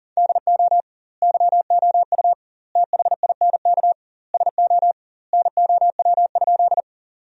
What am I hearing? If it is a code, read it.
Morse code, heard as a keyed tone, about 40 wpm, 700 Hz. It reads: DO YOU THINK SO NOW?
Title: 2x Word-Spacing - Sentences from Top 200 Words - 40wpm